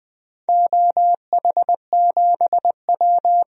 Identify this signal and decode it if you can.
Morse code, heard as a keyed tone, about 20 wpm, 700 Hz. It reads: OH7W
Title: DXCC-100 Call Signs - 4 Characters - Repeated 2x - 20wpm